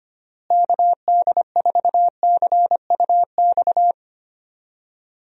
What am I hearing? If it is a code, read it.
Morse code, heard as a keyed tone, about 25 wpm, 700 Hz. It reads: KD4CUX
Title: USA Call Signs - 6 Characters - Repeated 2x - 25wpm